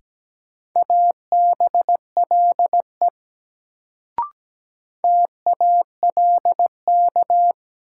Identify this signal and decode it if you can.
Morse code, heard as a keyed tone, about 17 wpm, 700 Hz. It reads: ABLE TALK
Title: Sets of 2 Words - New Word Drill - 17wpm